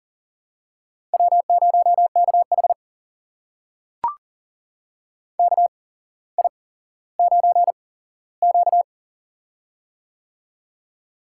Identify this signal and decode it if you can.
Morse code, heard as a keyed tone, about 39 wpm, spaced as for 17 wpm, 700 Hz. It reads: W0KH KI9Q
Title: USA Call Signs - 8x Character Spacing - 40wpm